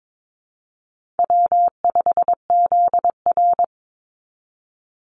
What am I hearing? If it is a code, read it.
Morse code, heard as a keyed tone, about 22 wpm, 700 Hz. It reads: W5ZR
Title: Easy Call Signs Repeated 3x - 22wpm